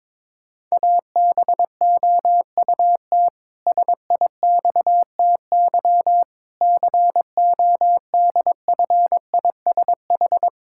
Morse code, heard as a keyed tone, about 22 wpm, 700 Hz. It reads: ABOUT SIXTY CODFISH
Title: Pangram Challenge v2 - 22wpm